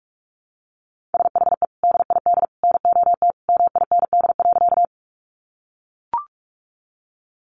Morse code, heard as a keyed tone, about 44 wpm, 700 Hz. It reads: SHE DID NOT MIND.
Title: Sentences from Top 600 Words 45wpm